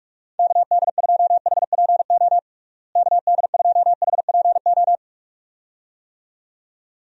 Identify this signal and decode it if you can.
Morse code, heard as a keyed tone, about 46 wpm, 700 Hz. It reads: KD2HPO KD2HPO
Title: USA Call Signs - 6 Characters - Repeated 2x - 45wpm